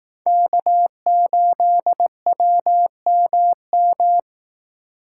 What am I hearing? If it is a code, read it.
Morse code, heard as a keyed tone, about 18 wpm, 700 Hz. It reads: K8WMM
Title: Speed Racing - USA 1x3 Call Signs 15wpm